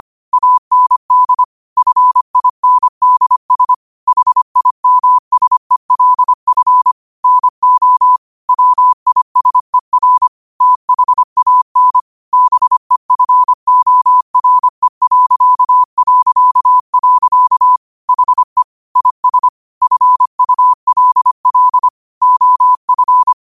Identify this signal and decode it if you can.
Morse code, heard as a keyed tone, about 25 wpm, 1000 Hz. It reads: AND FINDS HIMSELF NO WISER THAN BEFORE... HE IS FULL OF
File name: 25wpm